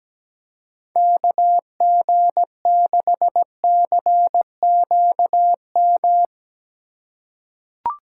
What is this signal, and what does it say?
Morse code, heard as a keyed tone, about 17 wpm, 700 Hz. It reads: KG6CQM
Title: USA Call Signs - 6 Characters - 6x Character Spacing - 17wpm